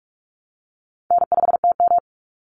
Morse code, heard as a keyed tone, about 44 wpm, 700 Hz. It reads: N5TM
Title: Easy Call Signs Repeated 3x - 45wpm